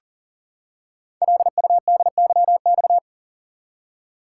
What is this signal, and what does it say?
Morse code, heard as a keyed tone, about 40 wpm, 690 Hz. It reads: LUDYX
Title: Code Groups - 5 Letters - 40wpm